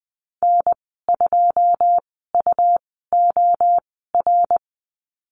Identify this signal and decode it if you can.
Morse code, heard as a keyed tone, about 20 wpm, spaced as for 14 wpm, 700 Hz. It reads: N2UOR
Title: 147 Lesson 24g - Letter y - Call Signs 20wpm